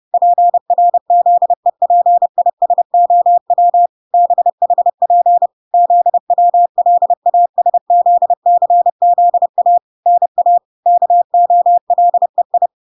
Morse code, heard as a keyed tone, about 30 wpm, 700 Hz. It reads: PRZEPISOW BHP ZWLASZCZA NA KOLEI